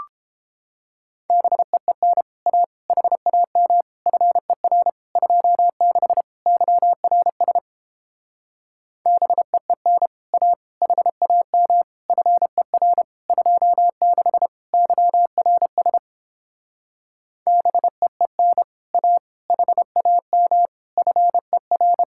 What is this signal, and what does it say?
Morse code, heard as a keyed tone, about 31 wpm, 700 Hz. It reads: BEEN A HAM FER 26 YRS BEEN A HAM FER 26 YRS BEEN A HAM FER